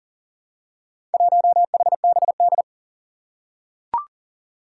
Morse code, heard as a keyed tone, about 41 wpm, 700 Hz. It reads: 1HBD
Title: Four Challenging Letter-Number Combinations 40wpm